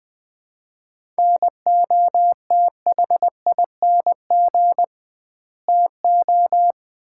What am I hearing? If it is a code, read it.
Morse code, heard as a keyed tone, about 20 wpm, 700 Hz. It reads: NOTHING TO
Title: 2x Word-Spacing - Sets of 4 Words - 20wpm